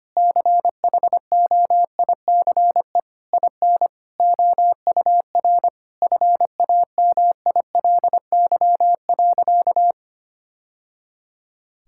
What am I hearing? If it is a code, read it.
Morse code, heard as a keyed tone, about 25 wpm, 700 Hz. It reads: CHOICE IN OUR FAMILY.
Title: Sentences from Top 1000 Words - Review Part 6 - 25wpm